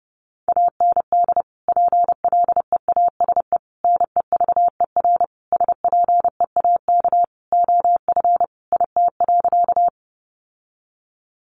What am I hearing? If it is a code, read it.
Morse code, heard as a keyed tone, about 28 wpm, 700 Hz. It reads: AND PLEASE NEVER SPEAK OF IT.
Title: Sentences from Top 1500 Words 30wpm